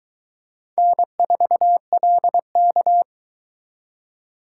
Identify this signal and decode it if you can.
Morse code, heard as a keyed tone, about 21 wpm, 700 Hz. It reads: N4LK